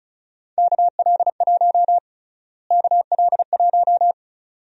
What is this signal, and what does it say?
Morse code, heard as a keyed tone, about 36 wpm, 700 Hz. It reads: KL1 KL1
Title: USA Call Sign Prefixes - Repeated 2x - 35wpm